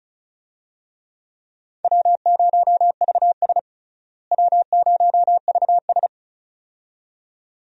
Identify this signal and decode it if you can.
Morse code, heard as a keyed tone, about 36 wpm, 700 Hz. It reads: W0VS W0VS
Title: USA Call Signs - 4 Characters - Repeated 2x - 35wpm